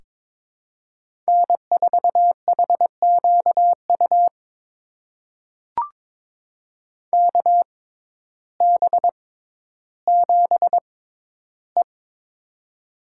Morse code, heard as a keyed tone, about 22 wpm, spaced as for 12 wpm, 700 Hz. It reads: N4HQU KB7E
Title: USA Call Signs - 6x Character Spacing - 22wpm